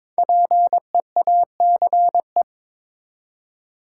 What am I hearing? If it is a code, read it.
Morse code, heard as a keyed tone, about 22 wpm, 700 Hz. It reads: PEACE
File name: 5 Letter Words - First 3 Letters Given - 22wpm